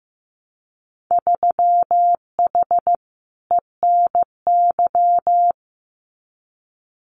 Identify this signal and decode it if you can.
Morse code, heard as a keyed tone, about 15 wpm, 700 Hz. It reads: 3H ENY
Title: Winter Field Day Exchanges Encore 15wpm